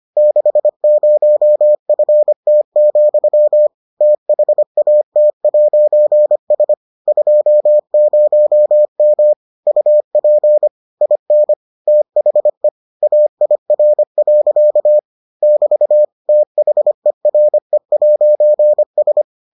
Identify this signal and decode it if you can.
Morse code, heard as a keyed tone, about 25 wpm, 600 Hz. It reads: 60FT, THAT'S 20M UP IN THE AIR. <BT> THERE'S